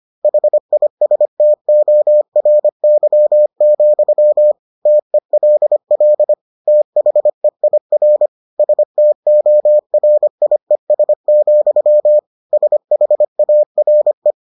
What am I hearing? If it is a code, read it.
Morse code, heard as a keyed tone, about 25 wpm, 600 Hz. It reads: HISTORY, TELL THEIR STORIES, SHARE